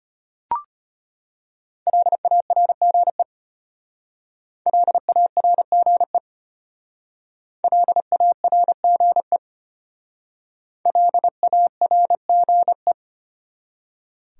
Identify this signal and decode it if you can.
Morse code, heard as a keyed tone, about 32 wpm, 700 Hz. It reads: LARGE LARGE LARGE LARGE